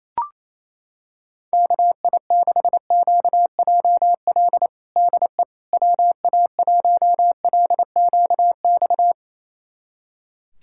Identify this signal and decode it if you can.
Morse code, heard as a keyed tone, about 28 wpm, 700 Hz. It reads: KI6QJL DE WA1LQX
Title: USA Callsign DE Callsign - 6 Characters - 28wpm